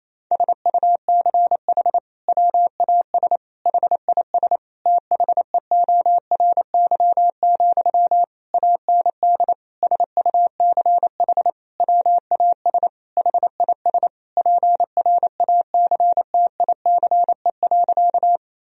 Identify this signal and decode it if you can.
Morse code, heard as a keyed tone, about 28 wpm, 700 Hz. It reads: SUCH WAS HIS THEORY, AND SUCH WAS HIS PRACTICE.
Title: Sentences from Top 1000 Words - Review Part 4 - 28wpm